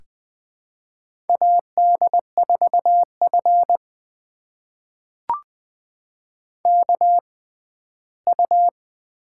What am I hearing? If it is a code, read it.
Morse code, heard as a keyed tone, about 20 wpm, spaced as for 14 wpm, 700 Hz. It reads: AD4F KU
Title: USA Call Signs - 4 Characters - 6x Character Spacing - 20wpm